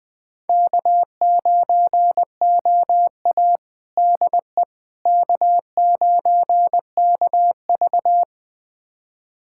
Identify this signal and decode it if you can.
Morse code, heard as a keyed tone, about 20 wpm, 700 Hz. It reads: K9OA DE K9KV
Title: USA Callsign DE Callsign - 4 Characters - 20wpm